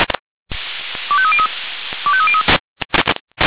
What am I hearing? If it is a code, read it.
Five-tone selective call (ZVEI), heard as quick runs of five tones, tones of about 70 ms. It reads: 25303 25303